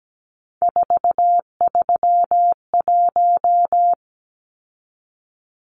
Morse code, heard as a keyed tone, about 17 wpm, 700 Hz. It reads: 431
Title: Numbers - 3-Digit 17wpm